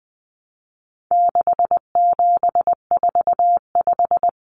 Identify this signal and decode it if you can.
Morse code, heard as a keyed tone, about 20 wpm, 700 Hz. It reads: 6745
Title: Common Cut Numbers - 4-Digit 20wpm